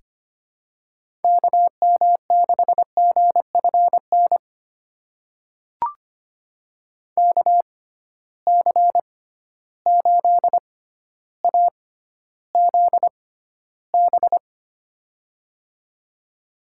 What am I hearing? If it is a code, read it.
Morse code, heard as a keyed tone, about 25 wpm, spaced as for 13 wpm, 700 Hz. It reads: KM6GFN KC8AZB
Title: USA Call Signs - 6 Characters - 6x Character Spacing - 25wpm